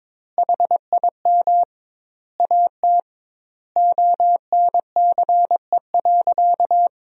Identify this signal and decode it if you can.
Morse code, heard as a keyed tone, about 22 wpm, 700 Hz. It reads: HIM AT ONCE.
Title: 2x Word-Spacing - Sentences from Top 300 Words - 22wpm